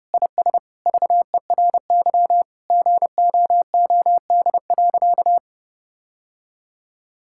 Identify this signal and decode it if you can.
Morse code, heard as a keyed tone, about 32 wpm, 700 Hz. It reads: IS VERY GOOD.
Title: Sentences from Top 1000 Words - Review Part 6 - 30wpm